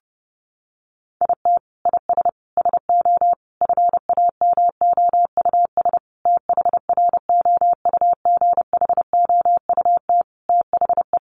Morse code, heard as a keyed tone, about 30 wpm, 700 Hz. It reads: IT IS SO FAMOUS THROUGHOUT THE
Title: Sentences from Top 1500 Words 30wpm